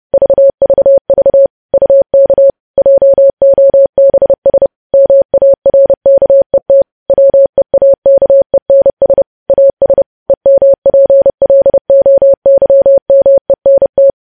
Morse code, heard as a keyed tone, about 30 wpm, 550 Hz. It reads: VVV UK JOBS MARKET WEAKENS AS EMPLOYMENT
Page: news headlines in morse code at 30 wpm